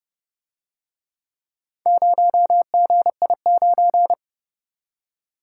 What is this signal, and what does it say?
Morse code, heard as a keyed tone, about 29 wpm, 700 Hz. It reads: 0GI9